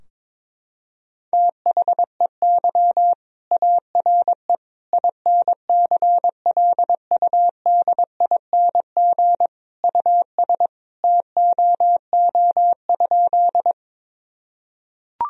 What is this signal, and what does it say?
Morse code, heard as a keyed tone, about 22 wpm, 700 Hz. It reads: THEY ARE INCLUDING US TOO?
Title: Sentences from Top 1000 Words - Review Part 4 - 22wpm